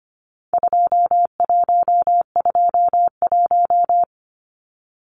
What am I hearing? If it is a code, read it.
Morse code, heard as a keyed tone, about 25 wpm, 700 Hz. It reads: 2121
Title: Numbers - 4-Digit 25wpm